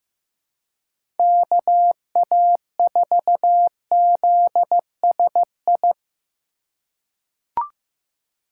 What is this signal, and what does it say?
Morse code, heard as a keyed tone, about 15 wpm, 700 Hz. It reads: KA4ZSI
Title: USA Call Signs - 2x Character Spacing - 15wpm